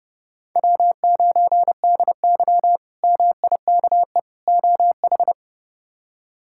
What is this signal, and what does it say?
Morse code, heard as a keyed tone, about 30 wpm, 700 Hz. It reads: W9DY MIKE OH
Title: K1USN SST QSO Practice - 30wpm